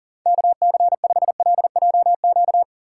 Morse code, heard as a keyed tone, about 51 wpm, 700 Hz. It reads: KC5LJQ